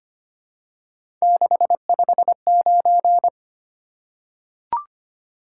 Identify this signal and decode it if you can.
Morse code, heard as a keyed tone, about 25 wpm, 700 Hz. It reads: 659
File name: Three Letter-Number Combinations - 25wpm